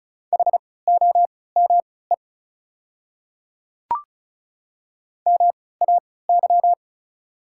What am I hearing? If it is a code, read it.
Morse code, heard as a keyed tone, about 36 wpm, spaced as for 20 wpm, 690 Hz. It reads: HOME MAY
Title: 3x Character Spacing - Top 500 Words - 35wpm